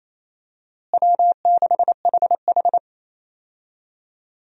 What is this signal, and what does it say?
Morse code, heard as a keyed tone, about 28 wpm, 700 Hz. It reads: W6HH
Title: USA Call Signs - 4 Characters - Repeated 2x - 28wpm